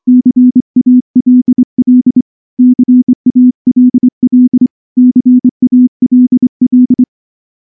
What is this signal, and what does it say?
Morse code, heard as a keyed tone, about 24 wpm, 250 Hz. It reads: CALL CALL CALL
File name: morse-training-25WPM